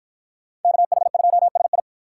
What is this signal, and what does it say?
Morse code, heard as a keyed tone, about 53 wpm, 700 Hz. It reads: KH2SI